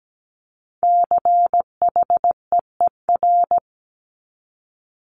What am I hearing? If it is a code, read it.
Morse code, heard as a keyed tone, about 17 wpm, 700 Hz. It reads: CHEER
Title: Multi-letter Phonemes Repeated 3x 17wpm